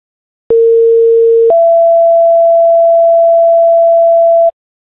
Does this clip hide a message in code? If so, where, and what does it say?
Yes: two-tone page from 0.5 to 4.5 s, heard as a one-second tone, then a three-second tone, about 460 then 670 Hz.